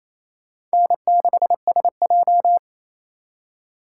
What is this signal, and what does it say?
Morse code, heard as a keyed tone, about 36 wpm, 700 Hz. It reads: N6SJ